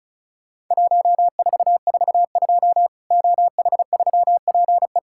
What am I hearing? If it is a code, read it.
Morse code, heard as a keyed tone, about 36 wpm, 700 Hz. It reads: 1442 OH3PE
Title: Worked All Europe DX Contest - QTC Exchange - 35wpm